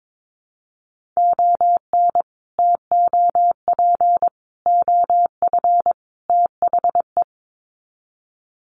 Morse code, heard as a keyed tone, about 22 wpm, 700 Hz. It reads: ON TOP OF THE
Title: Sets of 4 Words 22wpm